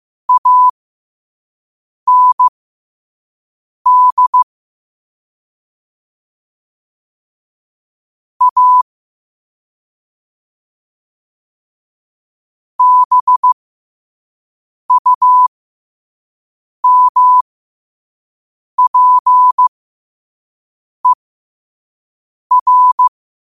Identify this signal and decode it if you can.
Morse code, heard as a keyed tone, about 15 wpm, spaced as for 5 wpm, 1000 Hz. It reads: AND A BUMPER
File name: qotdfiles / 5wpm